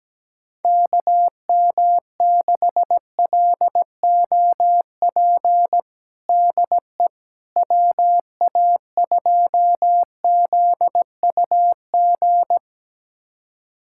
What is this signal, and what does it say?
Morse code, heard as a keyed tone, about 17 wpm, 700 Hz. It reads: KM6LOP DE WA2ZUG
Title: USA Callsign DE Callsign - 6 Characters - 17wpm